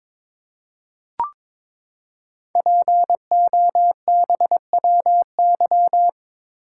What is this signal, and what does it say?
Morse code, heard as a keyed tone, about 22 wpm, 700 Hz. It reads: POBWY
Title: Code Groups - 5 Letters - 22wpm